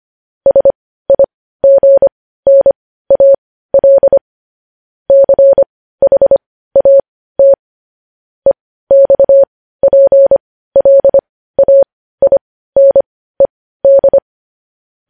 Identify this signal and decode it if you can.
Morse code, heard as a keyed tone, about 25 wpm, spaced as for 15 wpm, 550 Hz. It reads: SIGNAL CHAT EXPLAINED
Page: News Headlines in Morse Code at 15 WPM